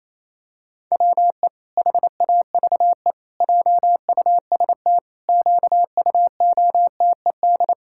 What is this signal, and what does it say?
Morse code, heard as a keyed tone, about 28 wpm, 700 Hz. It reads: WE HAVE JUST QUOTED
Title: Pangram Challenge v2 - 28wpm